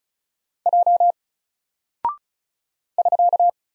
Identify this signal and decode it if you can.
Morse code, heard as a keyed tone, about 36 wpm, 700 Hz. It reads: J <SK>
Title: Single Character (Letter:Number:Punctuation:Prosign) 35wpm